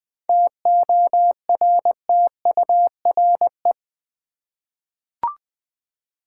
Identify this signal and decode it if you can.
Morse code, heard as a keyed tone, about 20 wpm, 700 Hz. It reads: TORTURE
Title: 7 Letter Words - First 3 Letters Given - 20wpm